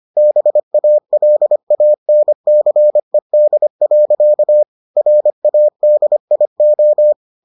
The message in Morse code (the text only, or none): BALANCED. RADIO